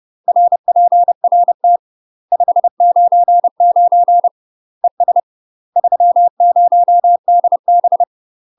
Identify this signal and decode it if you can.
Morse code, heard as a keyed tone, about 30 wpm, 700 Hz. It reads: RPRT 599 ES 30DB
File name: UK MASTER QSOs TRAINING / 30WPM / UK MASTER QSOs 0019